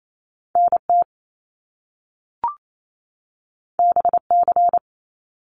Morse code, heard as a keyed tone, about 28 wpm, 700 Hz. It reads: NT BC
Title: Canadian Provinces and Territories 28wpm